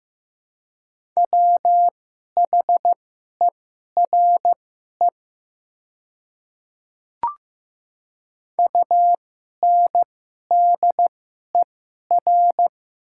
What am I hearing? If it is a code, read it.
Morse code, heard as a keyed tone, about 15 wpm, spaced as for 11 wpm, 700 Hz. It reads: WHERE UNDER
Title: 2x Character Spacing - Top 200 Words - 15wpm